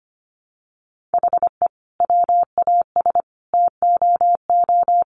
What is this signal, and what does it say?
Morse code, heard as a keyed tone, about 25 wpm, 700 Hz. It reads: HE WAS TOO
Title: Sentences from Top 600 Words 25wpm